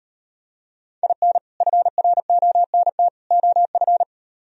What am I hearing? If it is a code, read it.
Morse code, heard as a keyed tone, about 38 wpm, 700 Hz. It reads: IN FRONT OF